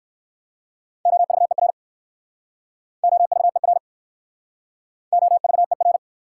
Morse code, heard as a keyed tone, about 54 wpm, 690 Hz. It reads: OVER OVER OVER
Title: Speed-Racing Prefixes and Suffixes 45wpm